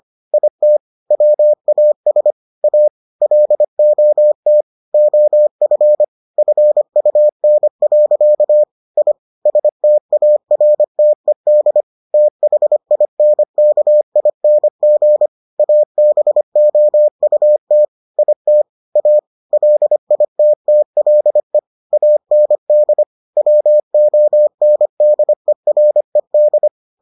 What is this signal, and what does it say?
Morse code, heard as a keyed tone, about 25 wpm, 600 Hz. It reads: IT WAS A LOT OF FUN. I STARTED THINKING ABOUT IT A LITTLE AND WONDERED